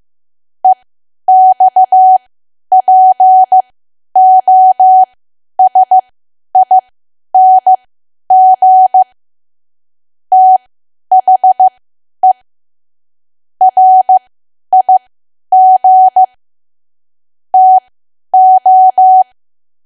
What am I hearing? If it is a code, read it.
Morse code, heard as a keyed tone, about 15 wpm, spaced as for 10 wpm, 750 Hz. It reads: EXPOSING THE RIG TO
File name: cwa / W1AW-Level2 / 10WPM